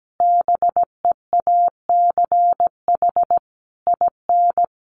Morse code, heard as a keyed tone, about 17 wpm, 700 Hz. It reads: BEACH IN
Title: Sentences from Top 1500 Words 17wpm